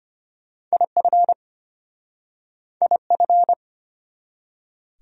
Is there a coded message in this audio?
Yes: Morse code from 0.7 to 3.5 s, heard as a keyed tone, about 33 wpm, 700 Hz. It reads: IF IF